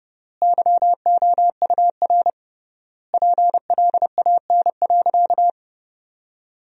Morse code, heard as a keyed tone, about 30 wpm, 700 Hz. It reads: YOUR PLAN.
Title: 3x Word-Spacing - Sentences from Top 500 Words - 30wpm